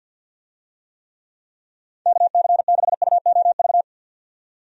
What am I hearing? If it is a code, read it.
Morse code, heard as a keyed tone, about 49 wpm, 700 Hz. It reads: KC6UOV
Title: USA Call Signs - 6 Characters - 50wpm